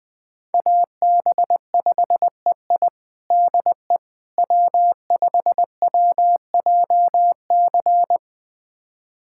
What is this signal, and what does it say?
Morse code, heard as a keyed tone, about 20 wpm, 700 Hz. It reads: AB5EI DE W5WJC
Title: USA Callsign DE Callsign - 5 Characters - 20wpm